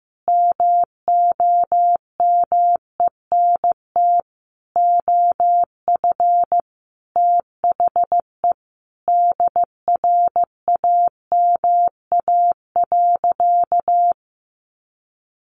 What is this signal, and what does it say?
Morse code, heard as a keyed tone, about 15 wpm, 700 Hz. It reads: MOMENT OF THE DRAMA.
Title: Sentences from Top 1500 Words 15wpm